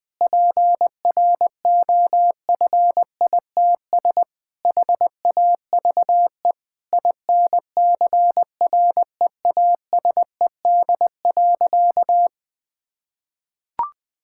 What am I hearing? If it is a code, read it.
Morse code, heard as a keyed tone, about 20 wpm, 700 Hz. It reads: PROFITS HAVE INCREASED.
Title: Sentences with 6-Word Phrases - 20wpm